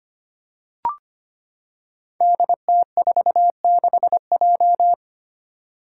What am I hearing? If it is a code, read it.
Morse code, heard as a keyed tone, about 25 wpm, 700 Hz. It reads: DT46J
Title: Five Challenging Letter-Number Combinations 25wpm